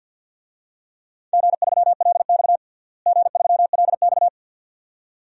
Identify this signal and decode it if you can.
Morse code, heard as a keyed tone, about 52 wpm, 700 Hz. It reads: G3LX G3LX
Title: DXCC-100 Call Signs - 4 Characters - Repeated 2x - 50wpm